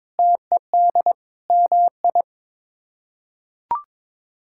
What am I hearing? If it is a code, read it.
Morse code, heard as a keyed tone, about 22 wpm, 700 Hz. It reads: TED MI
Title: K1USN SST QSO Practice - 22wpm